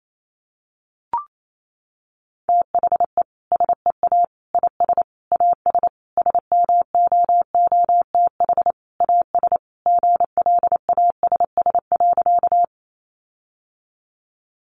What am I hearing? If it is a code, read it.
Morse code, heard as a keyed tone, about 28 wpm, 700 Hz. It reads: THE SEA IS AS SMOOTH AS GLASS.
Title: Sentences from Top 1400 Words 28wpm